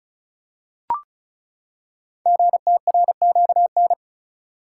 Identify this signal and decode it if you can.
Morse code, heard as a keyed tone, about 36 wpm, 700 Hz. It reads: GTRQN
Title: Code Groups - 5 Letters - 35wpm